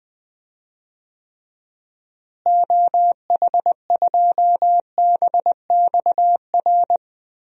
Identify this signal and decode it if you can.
Morse code, heard as a keyed tone, about 20 wpm, 700 Hz. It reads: OH2BXR
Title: DXCC-100 Call Signs - 6 Characters - Repeated 2x - 20wpm